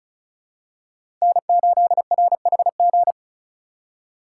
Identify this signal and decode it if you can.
Morse code, heard as a keyed tone, about 36 wpm, 700 Hz. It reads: N8RHG
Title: USA Call Signs - 5 Characters - 2x Character Spacing - 35wpm